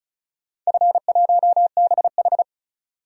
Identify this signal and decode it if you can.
Morse code, heard as a keyed tone, about 36 wpm, 700 Hz. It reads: F1BH
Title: DXCC-100 Call Signs - 4 Characters - 35wpm